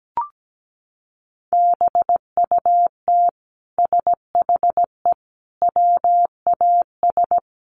Morse code, heard as a keyed tone, about 17 wpm, 700 Hz. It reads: BUT SHE WAS
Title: Sentences from Top 1500 Words 17wpm